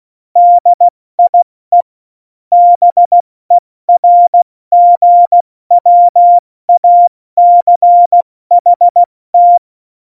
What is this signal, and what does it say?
Morse code, heard as a keyed tone, about 16 wpm, 700 Hz. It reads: DIE BERGWACHT
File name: Morsekurs / Witze / Witz 28. Januar 2020, 70 F80